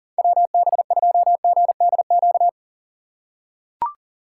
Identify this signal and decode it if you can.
Morse code, heard as a keyed tone, about 40 wpm, 700 Hz. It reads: WB2GDQ
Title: USA Call Signs - 6 Characters - 3x Character Spacing - 40wpm